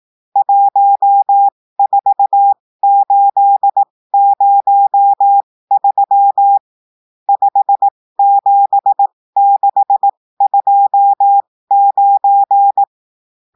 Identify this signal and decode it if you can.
Morse code, heard as a keyed tone, about 18 wpm, spaced as for 15 wpm, 800 Hz. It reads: 14803 57629
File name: cours-du-lundi files / cours-1580-texte-chiffre-15-mpm